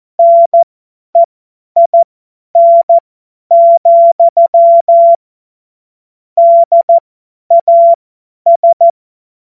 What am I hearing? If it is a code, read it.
Morse code, heard as a keyed tone, about 14 wpm, spaced as for 10 wpm, 680 Hz. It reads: NEIN, DAS